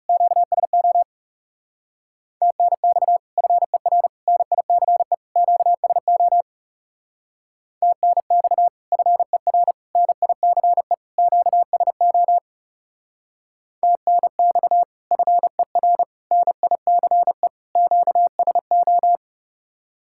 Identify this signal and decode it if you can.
Morse code, heard as a keyed tone, about 36 wpm, 700 Hz. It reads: QSO TNX FER NICE QSO TNX FER NICE QSO TNX FER NICE QSO
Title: Speed-Racing Common QSO Phrases - 30wpm